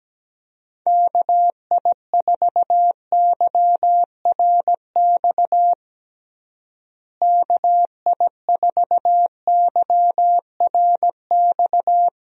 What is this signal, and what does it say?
Morse code, heard as a keyed tone, about 17 wpm, 700 Hz. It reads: KI4YRX KI4YRX
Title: USA Call Signs - 6 Characters - Repeated 2x - 17wpm